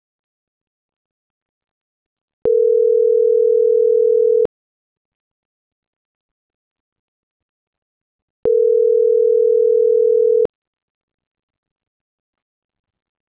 Ringback tone (North America), 440 + 480 Hz.